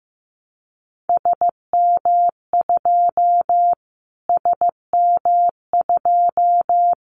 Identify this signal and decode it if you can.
Morse code, heard as a keyed tone, about 15 wpm, 700 Hz. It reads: SM2 SM2
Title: Callsign Prefixes for Top 25 DXCC - Repeated 3x - 15wpm